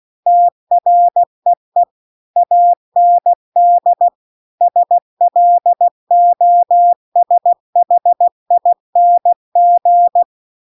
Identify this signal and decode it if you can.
Morse code, heard as a keyed tone, about 16 wpm, 700 Hz. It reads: TREE AND SLOSHING
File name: CW-The-Big-Twoheaded-River-2-16WPM